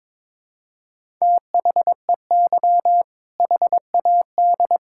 Morse code, heard as a keyed tone, about 21 wpm, 700 Hz. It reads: THEY HAD
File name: Build A Phrase - 2-Word Phrases - 22wpm